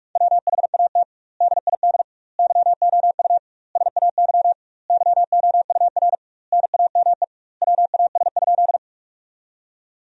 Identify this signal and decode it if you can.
Morse code, heard as a keyed tone, about 46 wpm, 700 Hz. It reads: WHAT DID YOU SAY YOUR NAME WAS?
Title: 2x Word-Spacing - Sentences from Top 200 Words - 45wpm